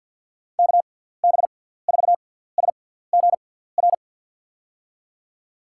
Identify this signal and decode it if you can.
Morse code, heard as a keyed tone, about 52 wpm, spaced as for 17 wpm, 700 Hz. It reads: KB4SGR
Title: USA Call Signs - 6 Characters - 6x Character Spacing - 50wpm